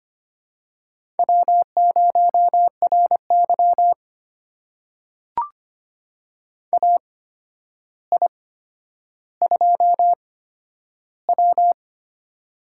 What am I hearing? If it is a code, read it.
Morse code, heard as a keyed tone, about 25 wpm, spaced as for 11 wpm, 700 Hz. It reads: W0RY AI2W
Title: USA Call Signs - 4 Characters - 8x Character Spacing - 25wpm